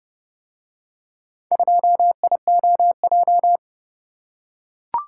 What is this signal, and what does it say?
Morse code, heard as a keyed tone, about 29 wpm, 700 Hz. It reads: 2IOJ